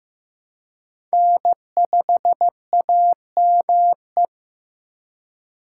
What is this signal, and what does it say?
Morse code, heard as a keyed tone, about 18 wpm, 700 Hz. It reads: N5AME
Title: Speed Racing - USA Call Signs 15wpm